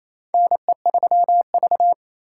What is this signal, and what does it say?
Morse code, heard as a keyed tone, about 28 wpm, 700 Hz. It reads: NE3V
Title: USA Call Signs - 4 Characters - 2x Character Spacing - 28wpm